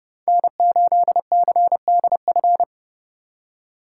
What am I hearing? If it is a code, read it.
Morse code, heard as a keyed tone, about 37 wpm, 700 Hz. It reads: N8CDF
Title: Speed Racing - USA 1x3 Call Signs 30wpm